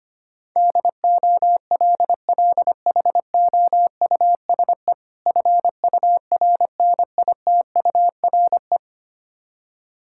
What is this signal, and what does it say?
Morse code, heard as a keyed tone, about 25 wpm, 700 Hz. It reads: DOLLHOUSE FURNITURE
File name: practice / 25wpm / Difficult Words with Hints - 25wpm